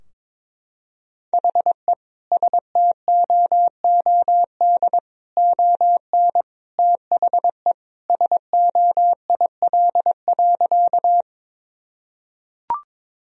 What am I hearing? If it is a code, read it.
Morse code, heard as a keyed tone, about 22 wpm, 700 Hz. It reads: HE STOOD ON THE SOIL.
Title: Sentences from Top 1000 Words - Review Part 4 - 22wpm